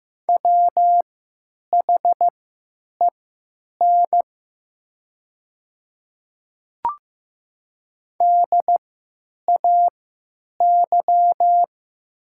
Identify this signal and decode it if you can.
Morse code, heard as a keyed tone, about 15 wpm, spaced as for 8 wpm, 700 Hz. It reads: WHEN DAY
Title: 3x Character Spacing - Top 200 Words - 15wpm